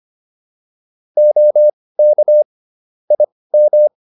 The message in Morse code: OK IM

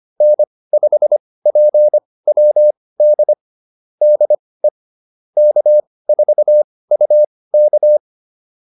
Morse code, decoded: N5PWD DE K4UK

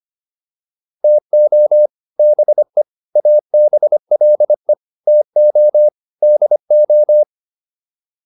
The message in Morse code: TO BE ABLE TO DO

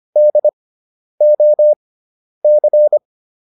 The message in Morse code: DOC

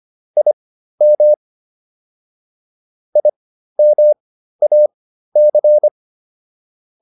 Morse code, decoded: IM IMAC

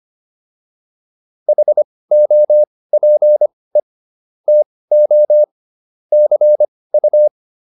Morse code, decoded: HOPE TO CU

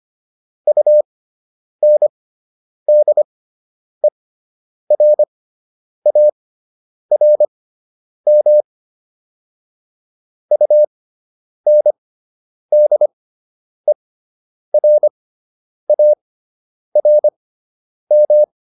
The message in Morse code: UNDERARM UNDERARM

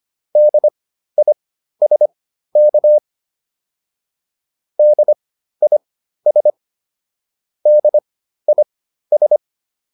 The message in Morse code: DISK DIS DIS

